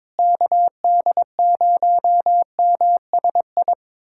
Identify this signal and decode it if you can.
Morse code, heard as a keyed tone, about 22 wpm, 700 Hz. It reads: KD0MSI
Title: USA Callsign DE Callsign - 6 Characters - 22wpm